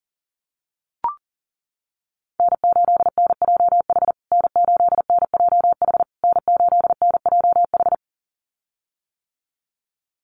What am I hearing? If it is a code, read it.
Morse code, heard as a keyed tone, about 40 wpm, 700 Hz. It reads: N8NJH N8NJH N8NJH
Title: USA Call Signs Repeated 3x 40wpm